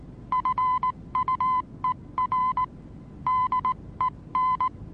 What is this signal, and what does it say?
Morse code, heard as a keyed tone, about 19 wpm, 1020 Hz. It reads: FUER DEN
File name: repos / Morsekurs / DJ4UF Morsen26